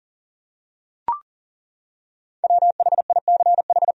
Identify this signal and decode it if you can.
Morse code, heard as a keyed tone, about 40 wpm, 700 Hz. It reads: WHICH
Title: Top 100 Words Repeated 40wpm